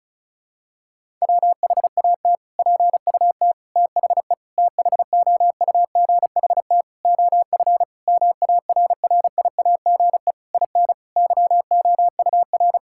Morse code, heard as a keyed tone, about 36 wpm, 700 Hz. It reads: WHAT PUT THE THOUGHT OF MARRIAGE IN YOUR
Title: Sentences from Top 1000 Words - Review Part 5 - 35wpm